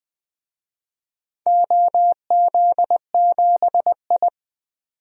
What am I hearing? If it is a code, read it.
Morse code, heard as a keyed tone, about 20 wpm, 700 Hz. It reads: OZ7I
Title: DXCC-100 Call Signs - 4 Characters - 20wpm